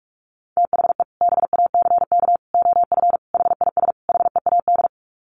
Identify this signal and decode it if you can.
Morse code, heard as a keyed tone, about 44 wpm, 700 Hz. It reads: THE BACK OF HIS HEAD
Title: Sets of 5 Words Encore 45wpm